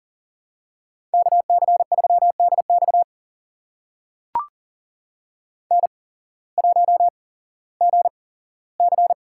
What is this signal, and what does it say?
Morse code, heard as a keyed tone, about 39 wpm, spaced as for 18 wpm, 700 Hz. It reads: KC3DX N1GC
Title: USA Call Signs - 5 Characters - 8x Character Spacing - 40wpm